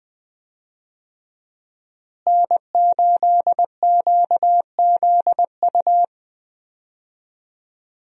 Morse code, heard as a keyed tone, about 20 wpm, 700 Hz. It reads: N8QZU
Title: USA Call Signs - 5 Characters - Repeated 3x - 20wpm